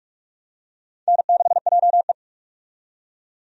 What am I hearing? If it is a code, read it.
Morse code, heard as a keyed tone, about 44 wpm, 700 Hz. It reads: N6JE
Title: USA Call Signs - 4 Characters - 45wpm